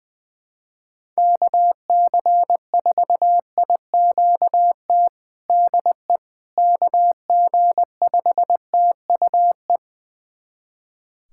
Morse code, heard as a keyed tone, about 20 wpm, 700 Hz. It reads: KC4IQT DE KG5TUE
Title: USA Callsign DE Callsign 20wpm